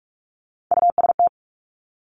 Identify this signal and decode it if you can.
Morse code, heard as a keyed tone, about 46 wpm, 700 Hz. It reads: UST
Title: Three Challenging Letter-Number Combinations 45wpm